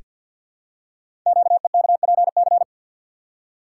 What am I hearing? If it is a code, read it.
Morse code, heard as a keyed tone, about 48 wpm, 690 Hz. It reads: QEKPC
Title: Code Groups - 5 Letters - 50wpm